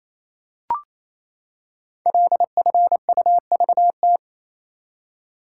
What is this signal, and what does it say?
Morse code, heard as a keyed tone, about 27 wpm, 700 Hz. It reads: LFUVT
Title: Code Groups - 5 Letters - 28wpm